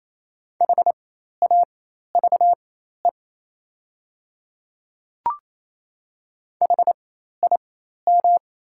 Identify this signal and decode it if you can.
Morse code, heard as a keyed tone, about 28 wpm, spaced as for 13 wpm, 700 Hz. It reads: HAVE HIM
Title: 4x Character Spacing - Top 100 Words - 28wpm